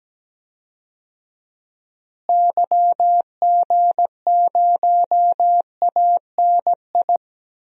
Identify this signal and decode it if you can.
Morse code, heard as a keyed tone, about 17 wpm, 700 Hz. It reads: YG0ANI
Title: DXCC-100 Call Signs - 6 Characters - Repeated 3x - 17wpm